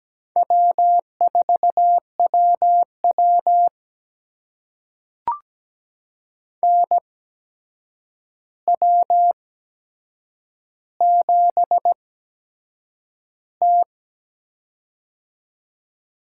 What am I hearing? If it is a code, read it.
Morse code, heard as a keyed tone, about 17 wpm, spaced as for 7 wpm, 700 Hz. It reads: W4WW NW7T
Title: USA Call Signs - 4 Characters - 8x Character Spacing - 17wpm